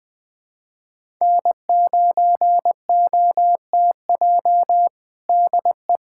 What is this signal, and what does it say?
Morse code, heard as a keyed tone, about 20 wpm, 700 Hz. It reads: N9OTJ DE
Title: USA Callsign DE Callsign - 5 Characters - 20wpm